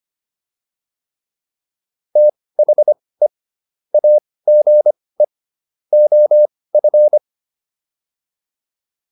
Morse code, heard as a keyed tone, about 25 wpm, spaced as for 18 wpm, 600 Hz. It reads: THE AGE OF